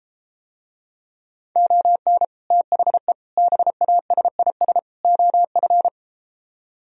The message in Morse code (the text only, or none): ON THE BASIS OF